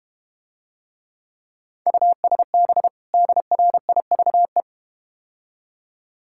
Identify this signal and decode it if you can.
Morse code, heard as a keyed tone, about 31 wpm, 700 Hz. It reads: USB DRIVE
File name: Easy Words - Speed-Racing with Hints - 25wpm